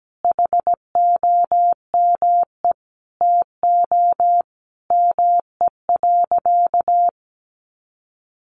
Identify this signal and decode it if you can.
Morse code, heard as a keyed tone, about 17 wpm, 700 Hz. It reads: HOME TO ME.
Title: Sentences from Top 1100 Words 17wpm